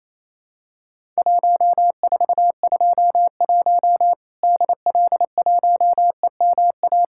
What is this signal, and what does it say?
Morse code, heard as a keyed tone, about 28 wpm, 700 Hz. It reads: 1421 DL1EMA
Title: Worked All Europe DX Contest - QTC Exchange - 28wpm